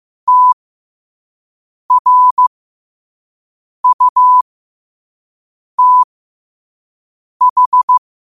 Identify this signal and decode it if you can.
Morse code, heard as a keyed tone, about 15 wpm, spaced as for 5 wpm, 1000 Hz. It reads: TRUTH
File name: qotdfiles / 5wpm